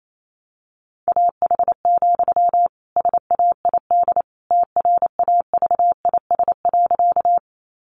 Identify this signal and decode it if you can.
Morse code, heard as a keyed tone, about 28 wpm, 700 Hz. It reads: AH, SAID TRAVIS.
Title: A Sound Of Thunder 28wpm